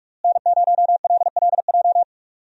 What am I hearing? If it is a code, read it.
Morse code, heard as a keyed tone, about 46 wpm, 700 Hz. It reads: N0LLJ
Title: USA Call Signs - 5 Characters - Repeated 2x - 45wpm